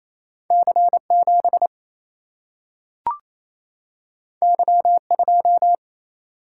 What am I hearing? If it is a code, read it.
Morse code, heard as a keyed tone, about 28 wpm, 700 Hz. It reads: C 7 Y 2
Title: Single-Letter Single-Number 28wpm